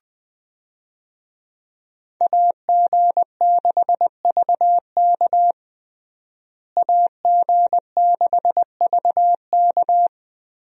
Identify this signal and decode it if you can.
Morse code, heard as a keyed tone, about 20 wpm, 700 Hz. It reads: AG6VK AG6VK
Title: USA Call Signs - 5 Characters - Repeated 3x - 20wpm